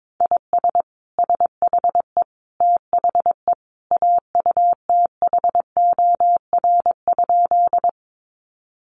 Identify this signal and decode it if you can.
Morse code, heard as a keyed tone, about 22 wpm, 700 Hz. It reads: IS SHE THE AUTHOR?
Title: Sentences from Top 1200 Words 22wpm